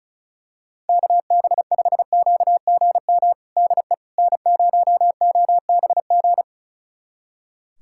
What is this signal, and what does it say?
Morse code, heard as a keyed tone, about 36 wpm, 700 Hz. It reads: KB5QGM DE N0OBG
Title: USA Callsign DE Callsign 35wpm